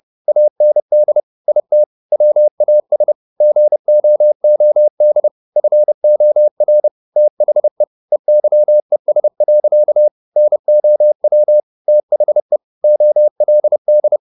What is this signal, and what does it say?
Morse code, heard as a keyed tone, about 30 wpm, 600 Hz. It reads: AND IT WAS GOOD FOR THE EYES. NOW THE OLD
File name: the old man and the sea / the old man and the sea 1 30wpm